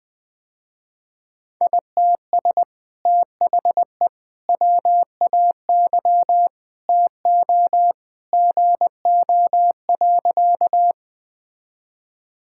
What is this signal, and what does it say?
Morse code, heard as a keyed tone, about 20 wpm, 700 Hz. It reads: ITS THE WAY TO GO.
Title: Sentences with 2-Word Phrases - Spoken Beginning - 20wpm